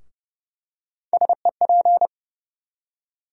Morse code, heard as a keyed tone, about 30 wpm, 700 Hz. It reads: SEP
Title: USA Call Sign Postfixes - 3 Letters 30wpm